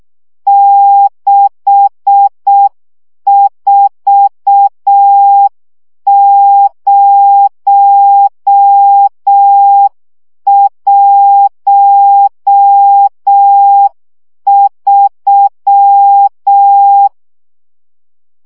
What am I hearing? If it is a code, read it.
Morse code, heard as a keyed tone, about 6 wpm, 800 Hz. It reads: 64013
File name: morze vizsga 2011-2 5